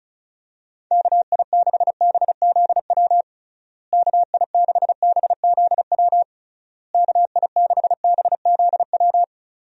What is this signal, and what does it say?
Morse code, heard as a keyed tone, about 34 wpm, 700 Hz. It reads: KI6BZW KI6BZW KI6BZW
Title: USA Call Signs - 6 Characters - Repeated 3x - 35wpm